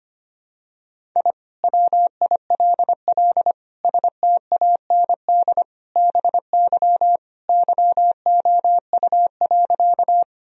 Morse code, heard as a keyed tone, about 25 wpm, 700 Hz. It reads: I WILL STAND BY YOU.
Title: Sentences from Top 1000 Words - Review Part 1 - 25wpm